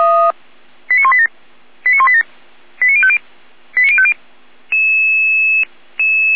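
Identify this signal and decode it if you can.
Five-tone selective call (ZVEI), heard as quick runs of five tones, tones of about 70 ms. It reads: 89187 89187 89050 89050
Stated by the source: ZVEI five-tone selective call received on SDR.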